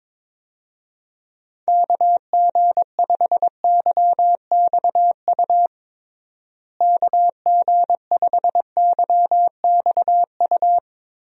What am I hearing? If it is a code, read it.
Morse code, heard as a keyed tone, about 22 wpm, 700 Hz. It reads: KG5YXU KG5YXU
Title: USA Call Signs - 6 Characters - Repeated 3x - 22wpm